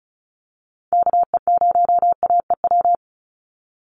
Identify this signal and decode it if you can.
Morse code, heard as a keyed tone, about 34 wpm, 700 Hz. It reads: KE0AEW
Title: USA Call Signs 35wpm